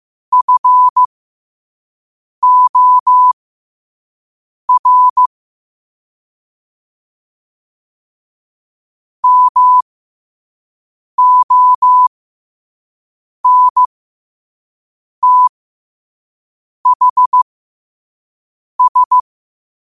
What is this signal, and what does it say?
Morse code, heard as a keyed tone, about 15 wpm, spaced as for 5 wpm, 1000 Hz. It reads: FOR MONTHS
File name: qotdfiles / 5wpm